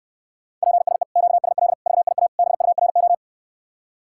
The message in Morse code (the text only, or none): PSE QSL VIA BURO